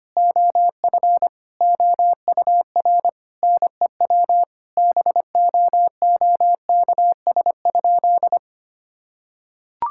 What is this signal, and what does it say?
Morse code, heard as a keyed tone, about 25 wpm, 700 Hz. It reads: OF OUR NEW BOOKS?
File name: practice / 25wpm / Sentences with 9-Word Phrases - 25wpm